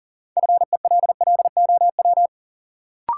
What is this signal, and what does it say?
Morse code, heard as a keyed tone, about 40 wpm, 700 Hz. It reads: FELLOW
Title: Sets of 5 Words Encore - New Word Drill - 40wpm